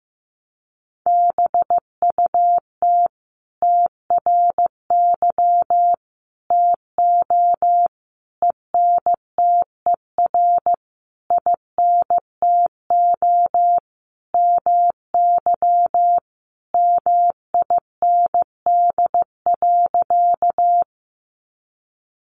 Morse code, heard as a keyed tone, about 15 wpm, 700 Hz. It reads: BUT TRY TO ENTER INTO MY MIND.
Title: Sentences from Top 1300 Words 15wpm